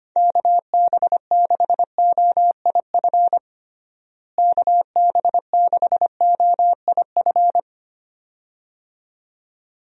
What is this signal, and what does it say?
Morse code, heard as a keyed tone, about 25 wpm, 700 Hz. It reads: KB6OIF KB6OIF
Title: USA Call Signs - 6 Characters - Repeated 2x - 25wpm